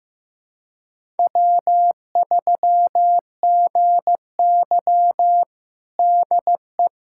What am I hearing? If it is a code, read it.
Morse code, heard as a keyed tone, about 15 wpm, 700 Hz. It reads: W3GY DE
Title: USA Callsign DE Callsign - 4 Characters - 15wpm